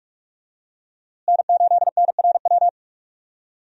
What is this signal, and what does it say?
Morse code, heard as a keyed tone, about 41 wpm, 700 Hz. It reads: N8NRW
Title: USA Call Signs - 5 Characters - Repeated 2x - 45wpm